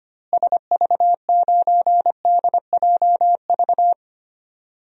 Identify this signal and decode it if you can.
Morse code, heard as a keyed tone, about 25 wpm, 700 Hz. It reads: SV9DJV
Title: DXCC-100 Call Signs - 6 Characters - Repeated 2x - 25wpm